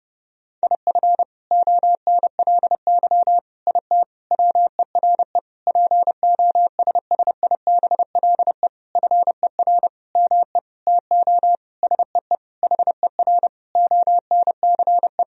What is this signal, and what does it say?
Morse code, heard as a keyed tone, about 30 wpm, 700 Hz. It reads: IF ONLY IT WERE POSSIBLE FER ME TO SEE HER ONCE
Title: Sentences from Top 500 Words and Abbreviations 30wpm